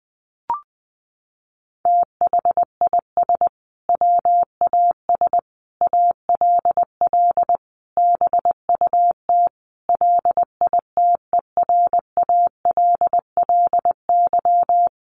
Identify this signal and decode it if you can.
Morse code, heard as a keyed tone, about 20 wpm, 700 Hz. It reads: THIS WAS ALL BUT LITERALLY
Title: Sentences from Top 1400 Words 20wpm